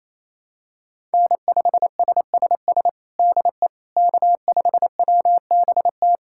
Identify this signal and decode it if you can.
Morse code, heard as a keyed tone, about 28 wpm, 700 Hz. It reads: N5SSS DE K5WBT
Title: USA Callsign DE Callsign - 5 Characters - 28wpm